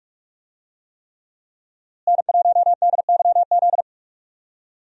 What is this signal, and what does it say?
Morse code, heard as a keyed tone, about 46 wpm, 700 Hz. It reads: N1DYZ
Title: USA Call Signs - 5 Characters - Repeated 3x - 45wpm